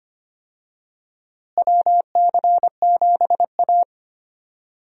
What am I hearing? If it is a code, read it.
Morse code, heard as a keyed tone, about 25 wpm, 700 Hz. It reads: WC7A
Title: USA Call Signs - 4 Characters - Repeated 2x - 25wpm